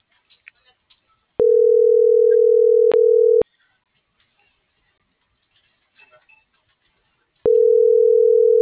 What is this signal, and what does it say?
Ringback tone (North America), 440 + 480 Hz.